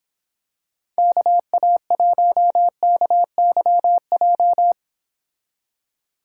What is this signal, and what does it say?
Morse code, heard as a keyed tone, about 26 wpm, 700 Hz. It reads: KA1KYJ